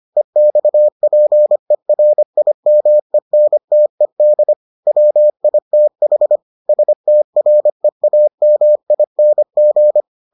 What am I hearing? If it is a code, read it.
Morse code, heard as a keyed tone, about 25 wpm, 600 Hz. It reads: EXPERIMENTED WITH STREAMING